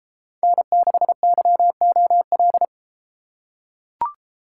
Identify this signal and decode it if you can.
Morse code, heard as a keyed tone, about 33 wpm, 700 Hz. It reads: N6YOL